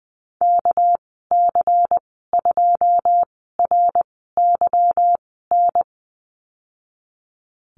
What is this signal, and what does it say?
Morse code, heard as a keyed tone, about 20 wpm, spaced as for 15 wpm, 700 Hz. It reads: KC2RYN
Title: 238 Lesson 38g - Letter q - Call Signs 20wpm